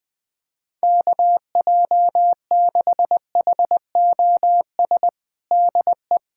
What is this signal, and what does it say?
Morse code, heard as a keyed tone, about 20 wpm, 700 Hz. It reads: KJ6HOS DE
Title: USA Callsign DE Callsign - 6 Characters - 20wpm